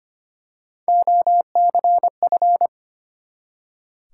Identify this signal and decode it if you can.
Morse code, heard as a keyed tone, about 25 wpm, 700 Hz. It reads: OCF
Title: Top 180 QSO Elements 25wpm